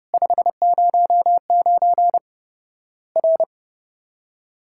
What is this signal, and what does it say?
Morse code, heard as a keyed tone, about 30 wpm, 700 Hz. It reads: 509 R
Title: Worked All Europe DX Contest - QTC Exchange - 30wpm